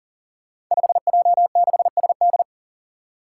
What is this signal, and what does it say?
Morse code, heard as a keyed tone, about 40 wpm, 700 Hz. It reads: 5J6SD
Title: Five Challenging Letter-Number Combinations 40wpm